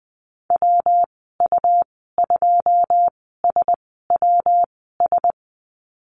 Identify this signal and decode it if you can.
Morse code, heard as a keyed tone, about 20 wpm, spaced as for 15 wpm, 700 Hz. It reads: WU2SWS
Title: 147 Lesson 24g - Letter y - Call Signs 20wpm